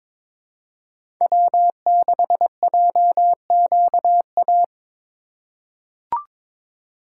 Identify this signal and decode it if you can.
Morse code, heard as a keyed tone, about 21 wpm, 700 Hz. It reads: W6JQA
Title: USA Call Signs - 5 Characters - 3x Character Spacing - 22wpm